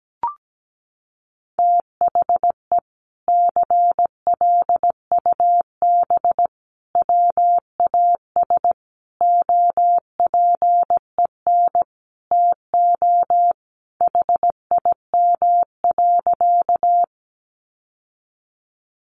Morse code, heard as a keyed tone, about 17 wpm, 700 Hz. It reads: THE CLUB WAS OPEN TO HIM.
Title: Sentences from Top 1200 Words 17wpm